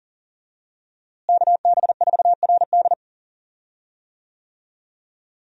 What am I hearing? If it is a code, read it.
Morse code, heard as a keyed tone, about 40 wpm, 700 Hz. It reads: KB4RD
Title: USA Call Signs - 5 Characters - Repeated 2x - 40wpm